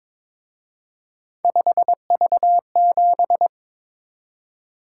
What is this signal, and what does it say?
Morse code, heard as a keyed tone, about 26 wpm, 700 Hz. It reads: 5V7
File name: Speed-Racing - Three Letter-Number Combinations - 22wpm